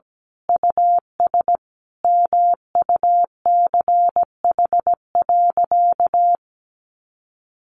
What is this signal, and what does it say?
Morse code, heard as a keyed tone, about 17 wpm, 700 Hz. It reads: US MUCH.
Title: Sentences from Top 1500 Words 17wpm